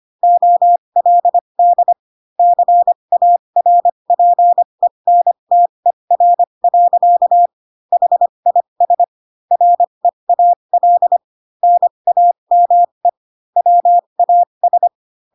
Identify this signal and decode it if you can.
Morse code, heard as a keyed tone, about 25 wpm, spaced as for 22 wpm, 700 Hz. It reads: OLD CARPENTER. HIS REAL NAME WAS